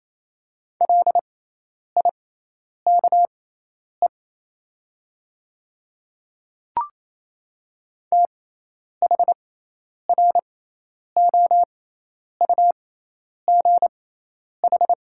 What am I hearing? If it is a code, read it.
Morse code, heard as a keyed tone, about 27 wpm, spaced as for 10 wpm, 700 Hz. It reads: LIKE THROUGH
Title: 6x Character Spacing - Top 200 Words - 28wpm